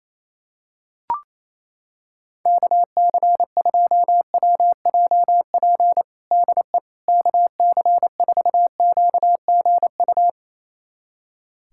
Morse code, heard as a keyed tone, about 28 wpm, 700 Hz. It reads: KC2WJP DE KC4QGU
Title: USA Callsign DE Callsign - 6 Characters - 28wpm